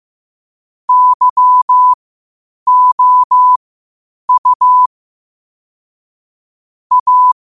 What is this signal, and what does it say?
Morse code, heard as a keyed tone, about 15 wpm, spaced as for 8 wpm, 1000 Hz. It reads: YOU A
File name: qotdfiles / 8wpm